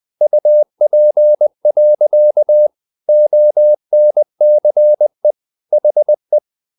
Morse code, heard as a keyed tone, about 20 wpm, 600 Hz. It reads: UP. ONCE HE
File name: the old man and the sea / the old man and the sea part 1 20wpm